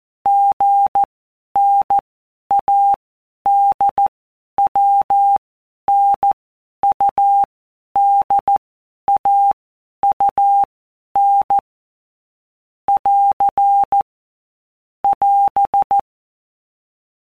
Morse code, heard as a keyed tone, about 14 wpm, 800 Hz. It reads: G N A D W N U D A U N <AR> <AS>
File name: Basic Letter Groups By Blocks / Groups@13 / AWUNDG@13